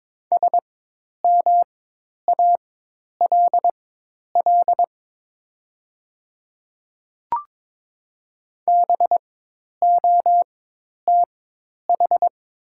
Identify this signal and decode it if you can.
Morse code, heard as a keyed tone, about 23 wpm, spaced as for 10 wpm, 700 Hz. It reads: SMALL BOTH
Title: 4x Character Spacing - Top 300 Words - 22wpm